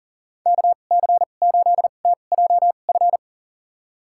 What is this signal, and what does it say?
Morse code, heard as a keyed tone, about 40 wpm, spaced as for 29 wpm, 700 Hz. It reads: KC8TJF
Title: USA Call Signs - 6 Characters - 2x Character Spacing - 40wpm